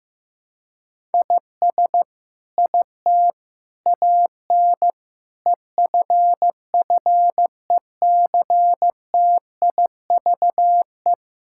Morse code, heard as a keyed tone, about 16 wpm, 700 Hz. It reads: IS IT AN EFFECTIVE
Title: Sentences from Top 1000 Words - Review Part 6 - 15wpm